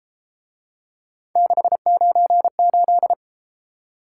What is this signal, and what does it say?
Morse code, heard as a keyed tone, about 32 wpm, 700 Hz. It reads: 698